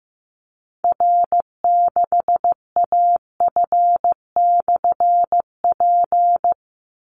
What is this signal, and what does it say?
Morse code, heard as a keyed tone, about 15 wpm, 700 Hz. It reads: R6AF/P
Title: Call Signs Repeated 3x - 15wpm